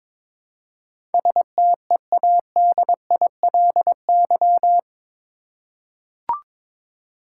Morse code, heard as a keyed tone, about 22 wpm, 700 Hz. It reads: STEADILY
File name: practice / 22wpm / Eight Letter Words - 22wpm